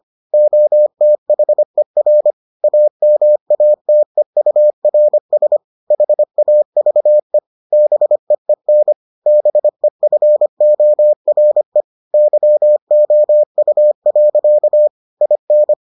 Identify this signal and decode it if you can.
Morse code, heard as a keyed tone, about 25 wpm, 600 Hz. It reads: OTHER AMATEURS HAVE BEEN BEFORE YOU. IN